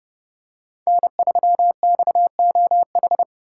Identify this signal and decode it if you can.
Morse code, heard as a keyed tone, about 29 wpm, 700 Hz. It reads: N3XOH